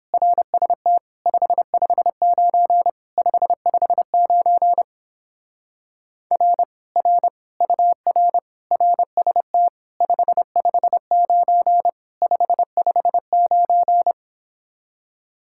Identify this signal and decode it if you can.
Morse code, heard as a keyed tone, about 29 wpm, 700 Hz. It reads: RST 559 559 R R UR RST 559 559